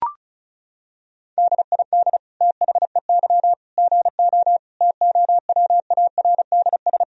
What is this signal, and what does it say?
Morse code, heard as a keyed tone, about 36 wpm, 700 Hz. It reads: DID THEY GO TOWARDS
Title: Sentences from Top 1000 Words - Review Part 2 - 35wpm